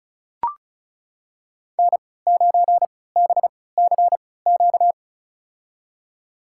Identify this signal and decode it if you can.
Morse code, heard as a keyed tone, about 36 wpm, spaced as for 20 wpm, 700 Hz. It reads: N9BCQ
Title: USA Call Signs - 5 Characters - 3x Character Spacing - 35wpm